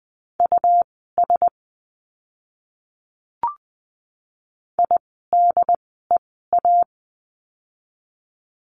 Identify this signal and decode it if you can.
Morse code, heard as a keyed tone, about 20 wpm, spaced as for 15 wpm, 700 Hz. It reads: US IDEA
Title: 108 Lesson 18f - Letter m - Words 20wpm